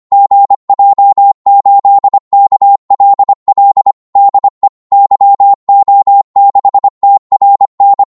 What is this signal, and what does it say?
Morse code, heard as a keyed tone, about 25 wpm, 800 Hz. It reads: GJ8KLL DE YO6TRN
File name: Morse / qso-25wpm